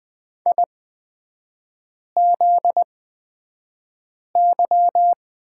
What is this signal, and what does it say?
Morse code, heard as a keyed tone, about 20 wpm, spaced as for 5 wpm, 700 Hz. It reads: IZY
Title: 243 Lesson 39e - Letter z - Three Characters 20wpm